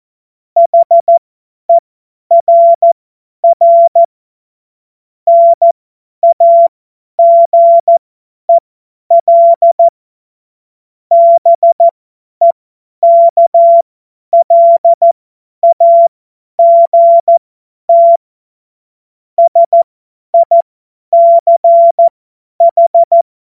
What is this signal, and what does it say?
Morse code, heard as a keyed tone, about 14 wpm, spaced as for 10 wpm, 680 Hz. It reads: HERR NAGEL BEKLAGT SICH
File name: Morsekurs / Witze / Witz 11. Juli 2023 50 F70